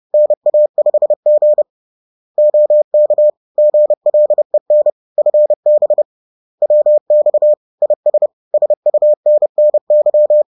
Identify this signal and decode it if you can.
Morse code, heard as a keyed tone, about 30 wpm, 600 Hz. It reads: NA5G OK GLEN FB WX IS SUNNY
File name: 30wpm-4